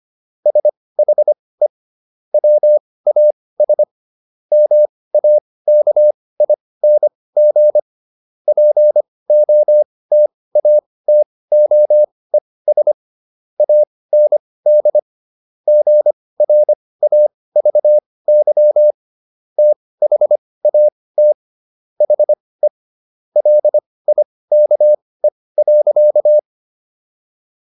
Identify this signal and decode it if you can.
Morse code, heard as a keyed tone, about 25 wpm, spaced as for 18 wpm, 600 Hz. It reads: SHE WAS MAKING POTATOES AND GRAVY THAT HE LIKE.